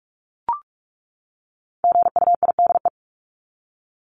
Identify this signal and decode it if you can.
Morse code, heard as a keyed tone, about 44 wpm, 700 Hz. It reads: GUIDE
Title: Top 501-750 Words 45wpm